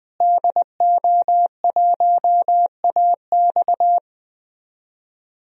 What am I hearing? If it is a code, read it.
Morse code, heard as a keyed tone, about 20 wpm, 700 Hz. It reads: DO1AX
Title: DXCC-100 Call Signs - 5 Characters - Repeated 3x - 20wpm